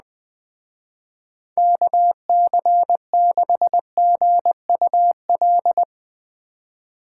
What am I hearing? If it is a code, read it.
Morse code, heard as a keyed tone, about 20 wpm, 700 Hz. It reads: KC6GUL
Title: Speed Racing - USA Call Signs 17wpm